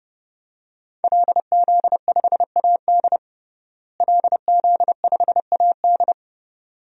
Morse code, heard as a keyed tone, about 32 wpm, 700 Hz. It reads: LZ5AD LZ5AD
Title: DXCC-100 Call Signs - 5 Characters - Repeated 2x - 30wpm